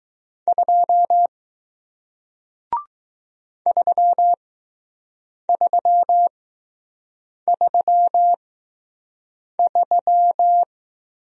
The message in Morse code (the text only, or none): 2 3333